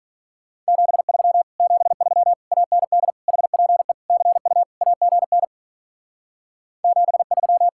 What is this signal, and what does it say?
Morse code, heard as a keyed tone, about 43 wpm, 700 Hz. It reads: 73 73 AND HPE CU AGN 73
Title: Speed-Racing Common QSO Phrases - 35wpm